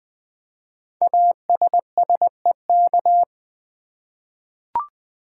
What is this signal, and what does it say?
Morse code, heard as a keyed tone, about 20 wpm, 700 Hz. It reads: ASSEK
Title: Code Groups - 5 Letters - 20wpm